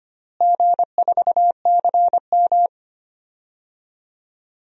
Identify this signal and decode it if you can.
Morse code, heard as a keyed tone, about 25 wpm, 700 Hz. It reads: G4CM
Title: DXCC-100 Call Signs - 4 Characters - Repeated 3x - 25wpm